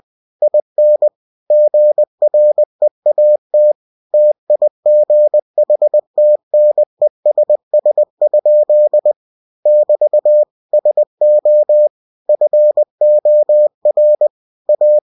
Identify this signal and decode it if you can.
Morse code, heard as a keyed tone, about 20 wpm, 600 Hz. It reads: IN GREAT TIGHTNESS? <BT> SO FOR A
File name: cw-books / winnie-the-pooh / winnie-20wpm-page-0019